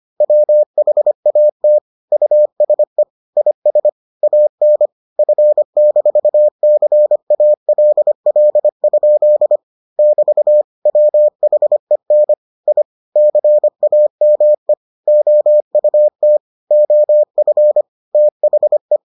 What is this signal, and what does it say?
Morse code, heard as a keyed tone, about 25 wpm, 600 Hz. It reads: WHAT USE IS AN F-CALL? <BT> WHEN I CAME OUT OF THE